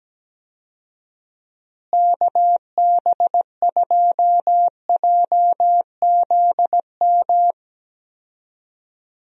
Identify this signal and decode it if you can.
Morse code, heard as a keyed tone, about 17 wpm, 700 Hz. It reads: KB2JZM
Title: USA Call Signs - 6 Characters - Repeated 3x - 17wpm